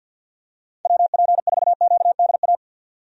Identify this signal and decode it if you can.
Morse code, heard as a keyed tone, about 48 wpm, 700 Hz. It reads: WP4QDA
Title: USA Call Signs - 6 Characters - 50wpm